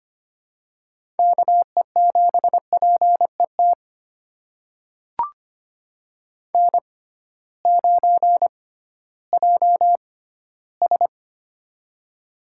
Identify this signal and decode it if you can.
Morse code, heard as a keyed tone, about 25 wpm, spaced as for 14 wpm, 700 Hz. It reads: KE7PET N9JS
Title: USA Call Signs - 6x Character Spacing - 25wpm